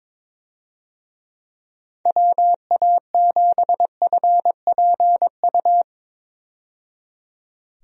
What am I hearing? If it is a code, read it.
Morse code, heard as a keyed tone, about 22 wpm, 700 Hz. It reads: WA7FPU
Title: USA Call Signs - 6 Characters - 22wpm